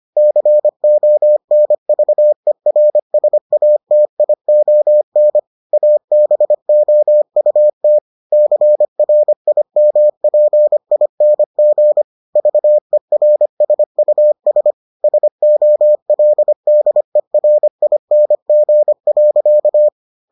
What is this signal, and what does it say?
Morse code, heard as a keyed tone, about 25 wpm, 600 Hz. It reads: CONVERSATION ABOUT CRIMPING VERSUS SOLDERING.